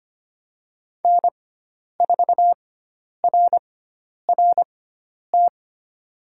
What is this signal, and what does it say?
Morse code, heard as a keyed tone, about 25 wpm, spaced as for 10 wpm, 700 Hz. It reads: N4RRT
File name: Lesson 16g - Letter c - Call Signs 25wpm